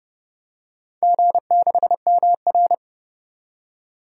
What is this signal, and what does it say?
Morse code, heard as a keyed tone, about 29 wpm, 700 Hz. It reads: G6MR
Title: DXCC-100 Call Signs - 4 Characters - 30wpm